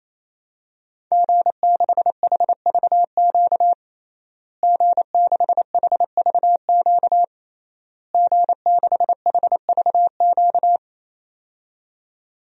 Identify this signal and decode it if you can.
Morse code, heard as a keyed tone, about 28 wpm, 700 Hz. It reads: G6HVQ G6HVQ G6HVQ
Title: DXCC-100 Call Signs - 5 Characters - Repeated 3x - 28wpm